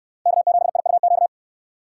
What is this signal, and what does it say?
Morse code, heard as a keyed tone, about 66 wpm, 700 Hz. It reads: K7IUQ